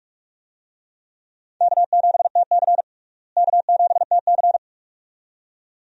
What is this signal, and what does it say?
Morse code, heard as a keyed tone, about 46 wpm, 700 Hz. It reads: K7TC K7TC
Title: USA Call Signs - 4 Characters - Repeated 2x - 45wpm